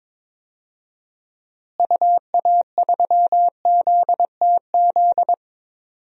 Morse code, heard as a keyed tone, about 21 wpm, 700 Hz. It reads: UA3ZTZ
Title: DXCC-100 Call Signs - 6 Characters - Repeated 2x - 22wpm